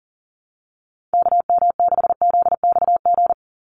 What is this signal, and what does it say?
Morse code, heard as a keyed tone, about 40 wpm, 700 Hz. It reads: KM6ZXG
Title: USA Call Signs 40wpm